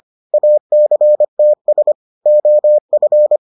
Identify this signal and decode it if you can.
Morse code, heard as a keyed tone, about 25 wpm, 600 Hz. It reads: ACTS OF